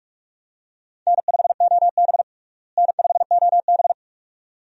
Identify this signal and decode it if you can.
Morse code, heard as a keyed tone, about 46 wpm, 700 Hz. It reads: N5OB N5OB
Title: USA Call Signs - 4 Characters - Repeated 2x - 45wpm